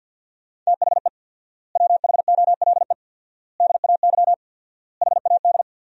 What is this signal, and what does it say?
Morse code, heard as a keyed tone, about 47 wpm, 700 Hz. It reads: THE WHOLE DAY HAD
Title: 4x Word-Spacing - Sentences from Top 500 Words - 50wpm